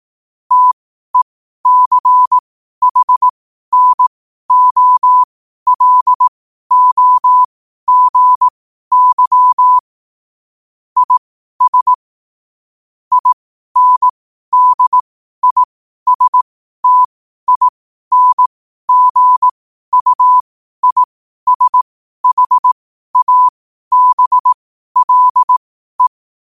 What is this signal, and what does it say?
Morse code, heard as a keyed tone, about 18 wpm, spaced as for 12 wpm, 1000 Hz. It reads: TECHNOLOGY IS INDISTINGUISHABLE
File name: qotdfiles / 12wpm